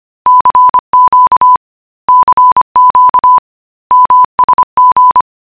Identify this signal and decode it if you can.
Morse code, heard as a keyed tone, about 25 wpm, 1000 Hz. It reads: CQ CQ MSG